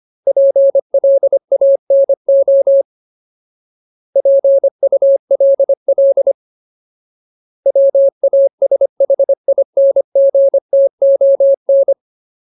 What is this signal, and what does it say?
Morse code, heard as a keyed tone, about 25 wpm, 550 Hz. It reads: PLANO PULL WASHINGTON
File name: Ultimate Word List / 25WPM Ultimate